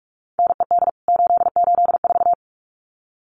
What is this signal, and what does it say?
Morse code, heard as a keyed tone, about 44 wpm, 700 Hz. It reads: NED 884
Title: CWT Exchange 45wpm